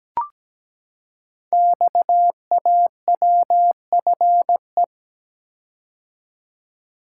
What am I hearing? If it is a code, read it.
Morse code, heard as a keyed tone, about 17 wpm, 700 Hz. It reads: XAWFE
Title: Code Groups - 5 Letters - 17wpm